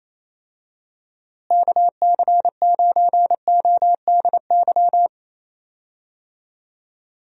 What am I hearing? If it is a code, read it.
Morse code, heard as a keyed tone, about 28 wpm, 700 Hz. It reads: KC9ODY
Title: USA Call Signs - 6 Characters - Repeated 3x - 28wpm